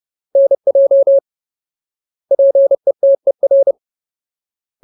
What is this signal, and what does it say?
Morse code, heard as a keyed tone, about 30 wpm, 550 Hz. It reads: NJ PETER
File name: Ultimate Word List / 30WPM Ultimate